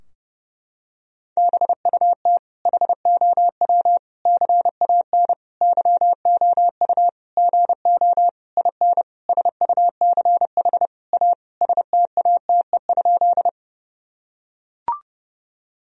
Morse code, heard as a keyed tone, about 32 wpm, 700 Hz. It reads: BUT HOW CAN YOU GO IN SUCH A STATE?
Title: Sentences from Top 1000 Words - Review Part 2 - 30wpm